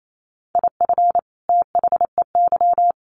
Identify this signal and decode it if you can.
Morse code, heard as a keyed tone, about 28 wpm, 700 Hz. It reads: IF THEY
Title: Sets of 2 Words 28wpm